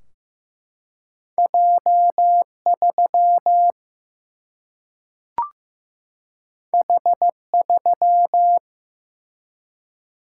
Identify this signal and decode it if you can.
Morse code, heard as a keyed tone, about 15 wpm, 700 Hz. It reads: J3 H3
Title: Single-Letter Single-Number 15wpm